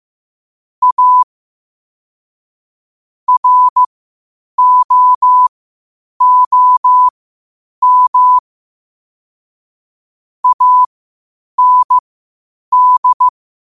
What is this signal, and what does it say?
Morse code, heard as a keyed tone, about 15 wpm, spaced as for 8 wpm, 1000 Hz. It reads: A ROOM AND
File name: qotdfiles / 8wpm